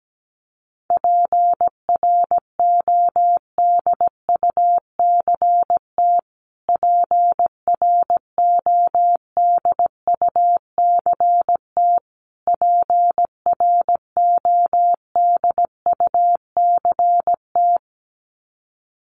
Morse code, heard as a keyed tone, about 17 wpm, 700 Hz. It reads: PRODUCT PRODUCT PRODUCT
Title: Top 500 Words Repeated 17wpm